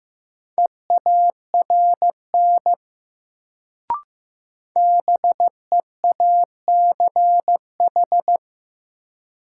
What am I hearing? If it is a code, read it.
Morse code, heard as a keyed tone, about 15 wpm, 700 Hz. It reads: EARN BEACH
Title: Sentences from Top 1500 Words - New Word Drill Alt - 15wpm